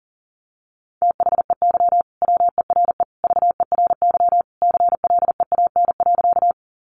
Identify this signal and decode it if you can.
Morse code, heard as a keyed tone, about 40 wpm, 700 Hz. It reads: THEY WERE VERY CLEAN.
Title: Sentences from Top 1100 Words 40wpm